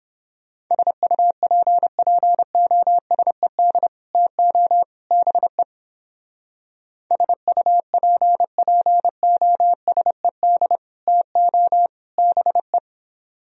Morse code, heard as a keyed tone, about 29 wpm, 700 Hz. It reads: SUPPOSED TO BE SUPPOSED TO BE